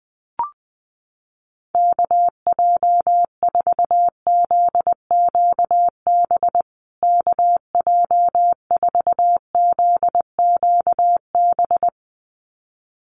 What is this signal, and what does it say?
Morse code, heard as a keyed tone, about 20 wpm, 700 Hz. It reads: KJ4ZQB KJ4ZQB
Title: USA Call Signs Repeated 2x 20wpm